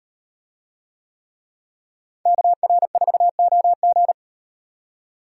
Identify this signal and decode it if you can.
Morse code, heard as a keyed tone, about 37 wpm, 700 Hz. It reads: KR4OG